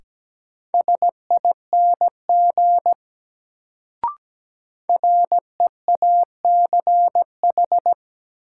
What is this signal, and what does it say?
Morse code, heard as a keyed tone, about 17 wpm, 700 Hz. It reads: SING REACH
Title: Sentences from Top 400 Words - New Word Drill - 17wpm